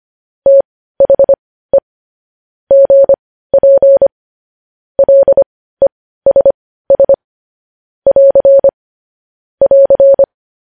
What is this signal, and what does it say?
Morse code, heard as a keyed tone, about 25 wpm, spaced as for 15 wpm, 550 Hz. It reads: THE GP LESS <AR> <AR>
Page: news headlines in morse code at 15 wpm